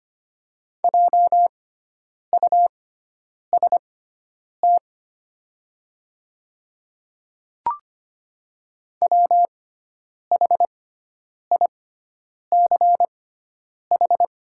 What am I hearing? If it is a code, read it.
Morse code, heard as a keyed tone, about 25 wpm, spaced as for 9 wpm, 690 Hz. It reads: JUST WHICH
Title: 6x Character Spacing - Top 200 Words - 25wpm